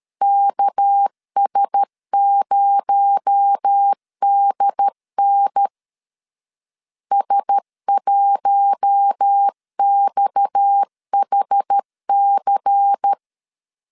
Morse code, heard as a keyed tone, about 13 wpm, 800 Hz. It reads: KS0DN S1XHC
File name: CW / DET / 1 6 1500C S A 12wpm 3440